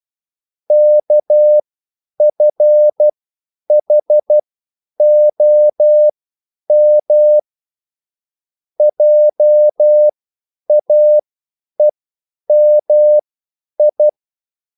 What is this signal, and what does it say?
Morse code, heard as a keyed tone, about 12 wpm, spaced as for 9 wpm, 600 Hz. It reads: KFHOM JAEMI
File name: QSL / CW lesson